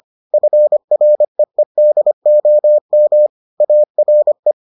Morse code, heard as a keyed tone, about 25 wpm, 600 Hz. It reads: FREEDOM ARE